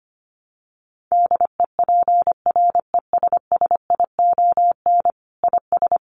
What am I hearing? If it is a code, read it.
Morse code, heard as a keyed tone, about 25 wpm, 700 Hz. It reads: DEPRESSION IS
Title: Sentences from Top 1300 Words 25wpm